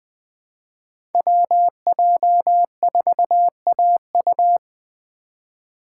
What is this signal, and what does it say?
Morse code, heard as a keyed tone, about 20 wpm, 700 Hz. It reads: WJ4AU
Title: Code Groups - 5 Letter:Numbers - 20wpm